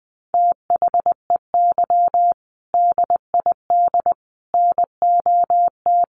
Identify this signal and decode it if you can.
Morse code, heard as a keyed tone, about 20 wpm, 700 Hz. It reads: THEY DID NOT
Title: Sets of 5 Words Encore2 20wpm